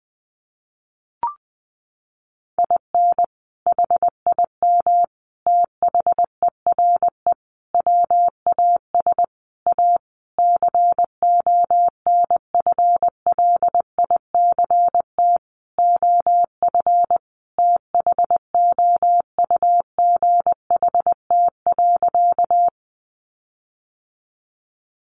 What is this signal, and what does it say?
Morse code, heard as a keyed tone, about 20 wpm, 700 Hz. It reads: IN HIM THERE WAS A CONFLICT OF THOUGHT.
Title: Sentences from Top 1300 Words 20wpm